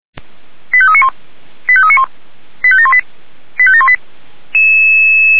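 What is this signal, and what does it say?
Five-tone selective call (ZVEI), heard as quick runs of five tones, tones of about 70 ms. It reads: 85391 85391 85718 85718
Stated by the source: ZVEI five-tone selective call received on SDR.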